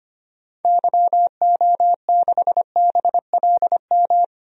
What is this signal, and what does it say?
Morse code, heard as a keyed tone, about 25 wpm, 700 Hz. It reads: YO6BLM
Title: DXCC-100 Call Signs - 6 Characters - 25wpm